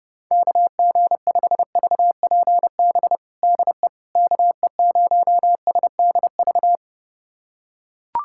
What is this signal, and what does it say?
Morse code, heard as a keyed tone, about 30 wpm, 700 Hz. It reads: KG5VPB DE KE0SDV
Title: USA Callsign DE Callsign 30wpm